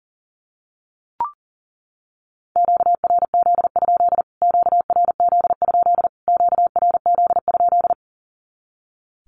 Morse code, heard as a keyed tone, about 40 wpm, 700 Hz. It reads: QRZ? QRZ? QRZ?
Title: Q-Codes Repeated 3x 40wpm